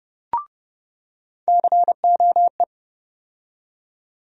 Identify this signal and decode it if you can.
Morse code, heard as a keyed tone, about 30 wpm, 700 Hz. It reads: COE
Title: USA Call Sign Postfixes - 3 Letters 30wpm